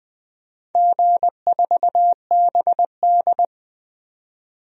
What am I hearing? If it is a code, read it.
Morse code, heard as a keyed tone, about 20 wpm, 700 Hz. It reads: G4BD
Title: DXCC-100 Call Signs - 4 Characters - 20wpm